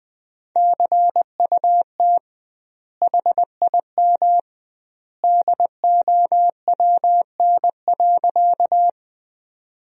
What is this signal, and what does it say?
Morse code, heard as a keyed tone, about 20 wpm, 700 Hz. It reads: CUT HIM DOWN.
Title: 2x Word-Spacing - Sentences from Top 300 Words - 20wpm